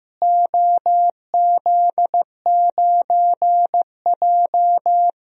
Morse code, heard as a keyed tone, about 15 wpm, 700 Hz. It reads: OZ9J
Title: DXCC-100 Call Signs - 4 Characters - Repeated 2x - 15wpm